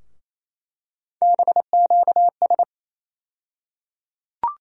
Morse code, heard as a keyed tone, about 27 wpm, 700 Hz. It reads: BQS
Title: USA Call Sign Postfixes - 3 Letters 28wpm